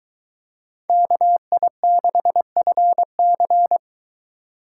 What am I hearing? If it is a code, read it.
Morse code, heard as a keyed tone, about 22 wpm, 700 Hz. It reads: KI6FC